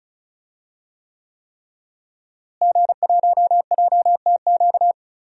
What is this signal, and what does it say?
Morse code, heard as a keyed tone, about 34 wpm, 700 Hz. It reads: G1JTQ
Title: DXCC-100 Call Signs - 5 Characters - Repeated 3x - 35wpm